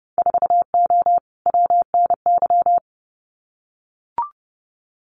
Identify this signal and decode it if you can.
Morse code, heard as a keyed tone, about 30 wpm, 700 Hz. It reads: 4O WNY
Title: Winter Field Day Exchanges Encore 30wpm